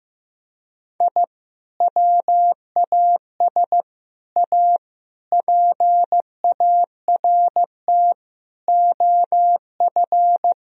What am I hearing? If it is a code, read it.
Morse code, heard as a keyed tone, about 15 wpm, 700 Hz. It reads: I WAS A PART OF
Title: Sentences from Top 1000 Words - Review Part 1 - 15wpm